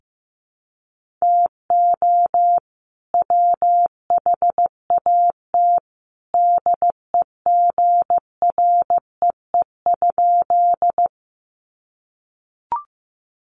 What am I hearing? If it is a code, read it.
Morse code, heard as a keyed tone, about 15 wpm, 700 Hz. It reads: TO WHAT DEGREE?
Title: Sentences from Top 1500 Words 15wpm